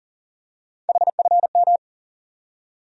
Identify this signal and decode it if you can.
Morse code, heard as a keyed tone, about 40 wpm, 700 Hz. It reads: HFM
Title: USA Call Sign Postfixes - 3 Letters 40wpm